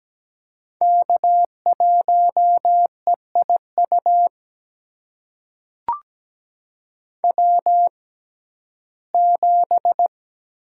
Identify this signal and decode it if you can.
Morse code, heard as a keyed tone, about 17 wpm, spaced as for 12 wpm, 700 Hz. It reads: K1EIU W7
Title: USA Call Signs - 5 Characters - 6x Character Spacing - 17wpm